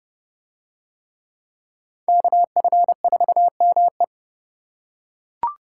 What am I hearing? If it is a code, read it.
Morse code, heard as a keyed tone, about 30 wpm, 700 Hz. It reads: KF4ME